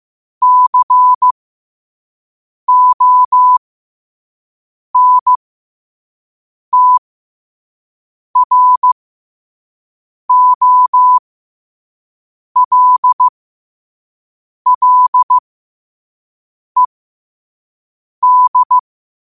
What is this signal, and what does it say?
Morse code, heard as a keyed tone, about 15 wpm, spaced as for 5 wpm, 1000 Hz. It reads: CONTROLLED
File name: qotdfiles / 5wpm